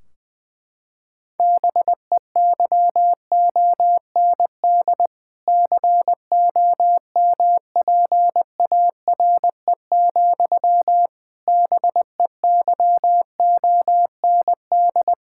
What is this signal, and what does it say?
Morse code, heard as a keyed tone, about 20 wpm, 700 Hz. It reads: BEYOND COMPARE, BEYOND
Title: Sentences from Top 1600 Words 20wpm